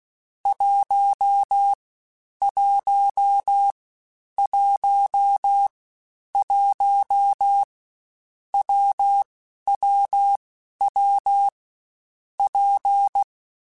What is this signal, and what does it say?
Morse code, heard as a keyed tone, about 16 wpm, 800 Hz. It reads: 1 1 1 1 W W W P
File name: Difficult Combos / Difficult Combos-mp3s / WPJ WPJ1 @15